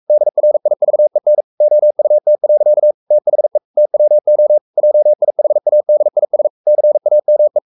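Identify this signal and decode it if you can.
Morse code, heard as a keyed tone, about 44 wpm, 600 Hz. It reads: DRIVEN OUT. THE TWO JIHADIS CAME
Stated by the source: Morse code at 43 wpm sending Ken Follett's novel Never in English, from CW med Gnister.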